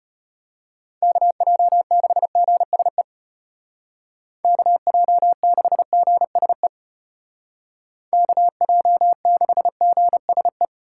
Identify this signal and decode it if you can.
Morse code, heard as a keyed tone, about 35 wpm, 700 Hz. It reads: KJ6GSE KJ6GSE KJ6GSE